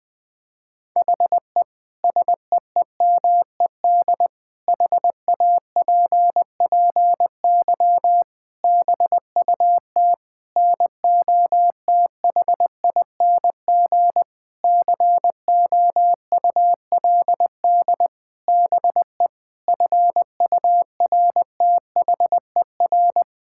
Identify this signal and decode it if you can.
Morse code, heard as a keyed tone, about 20 wpm, 700 Hz. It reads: HE SEEMED HAPPY BUT NOTHING COULD BE FURTHER
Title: Sentences with 8-Word Phrases - 20wpm